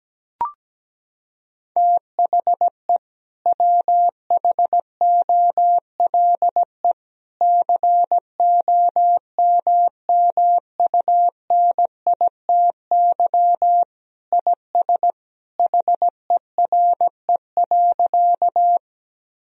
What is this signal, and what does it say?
Morse code, heard as a keyed tone, about 17 wpm, 700 Hz. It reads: THE WHOLE COMMUNITY IS HERE.
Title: Sentences from Top 1000 Words - Review Part 4 - 17wpm